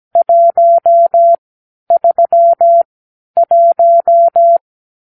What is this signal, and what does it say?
Morse code, heard as a keyed tone, about 17 wpm, spaced as for 10 wpm, 680 Hz. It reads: 131
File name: k7qo / Lesson 44 (1-3)